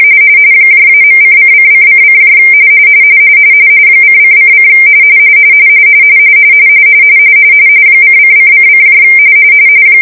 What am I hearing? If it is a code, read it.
RTTY, about 45.45 baud. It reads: -.:::::: :::: :::::